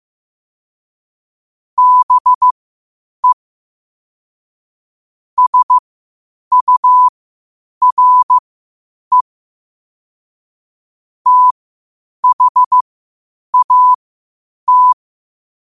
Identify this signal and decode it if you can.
Morse code, heard as a keyed tone, about 15 wpm, spaced as for 8 wpm, 1000 Hz. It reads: BE SURE THAT
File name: qotdfiles / 8wpm